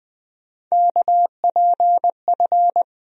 Morse code, heard as a keyed tone, about 20 wpm, 700 Hz. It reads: KPF
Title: USA Call Sign Postfixes - 3 Letters 20wpm